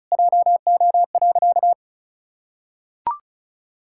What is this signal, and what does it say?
Morse code, heard as a keyed tone, about 34 wpm, 700 Hz. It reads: JO.
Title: Pangram Challenge v2 - 35wpm